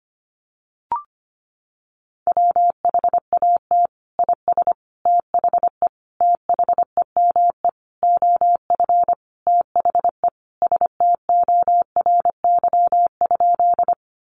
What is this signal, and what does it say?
Morse code, heard as a keyed tone, about 25 wpm, 700 Hz. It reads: WHAT IS THE THEME OF THE STORY?
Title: Sentences from Top 1100 Words 25wpm